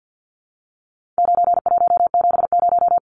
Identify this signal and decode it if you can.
Morse code, heard as a keyed tone, about 50 wpm, 700 Hz. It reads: 9170
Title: Numbers - 4-Digit 50wpm